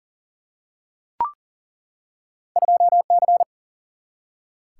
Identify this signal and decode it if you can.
Morse code, heard as a keyed tone, about 39 wpm, 700 Hz. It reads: 2C
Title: Single-Letter Single-Number 40wpm